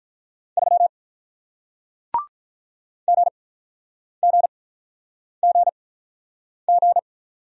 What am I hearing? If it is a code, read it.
Morse code, heard as a keyed tone, about 47 wpm, spaced as for 9 wpm, 700 Hz. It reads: 3 GGGG